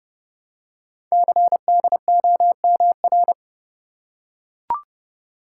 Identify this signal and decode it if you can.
Morse code, heard as a keyed tone, about 30 wpm, 700 Hz. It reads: CDOMR
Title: Code Groups - 5 Letters - 30wpm